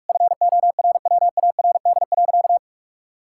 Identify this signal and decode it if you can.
Morse code, heard as a keyed tone, about 46 wpm, 700 Hz. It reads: FORWARD.
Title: Pangram Challenge v2 - 45wpm